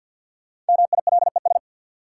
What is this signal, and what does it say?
Morse code, heard as a keyed tone, about 53 wpm, 700 Hz. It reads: MILES
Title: Sentences from Top 1000 Words - Review Part 3 - New Word Drill - 50wpm